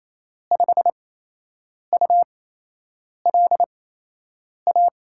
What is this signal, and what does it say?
Morse code, heard as a keyed tone, about 28 wpm, spaced as for 8 wpm, 700 Hz. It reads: 5ULA
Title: USA Call Signs - 6 Characters - 8x Character Spacing - 28wpm